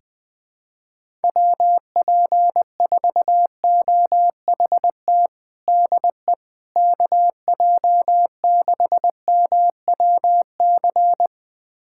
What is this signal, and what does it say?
Morse code, heard as a keyed tone, about 20 wpm, 700 Hz. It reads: WP4OHT DE KJ6MWC
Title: USA Callsign DE Callsign 20wpm